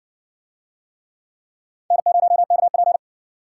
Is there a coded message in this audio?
Yes: Morse code from 1.9 to 3.0 s, heard as a keyed tone, about 55 wpm, 700 Hz. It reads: N0KP